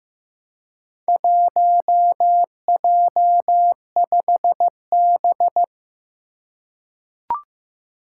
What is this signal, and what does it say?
Morse code, heard as a keyed tone, about 15 wpm, 700 Hz. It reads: 1J5B
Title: Four Challenging Letter-Number Combinations 15wpm